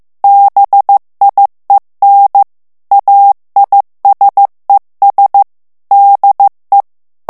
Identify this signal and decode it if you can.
Morse code, heard as a keyed tone, about 15 wpm, 800 Hz. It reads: BIEN AISES DE
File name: morse mp3 / FRANCAIS TEXTE CLAIR 2 VITESSE 14 MOTS MINUTES